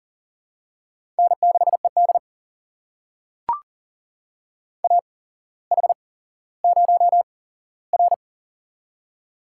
Morse code, heard as a keyed tone, about 38 wpm, spaced as for 17 wpm, 690 Hz. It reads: N6ED AH0R
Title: USA Call Signs - 4 Characters - 8x Character Spacing - 40wpm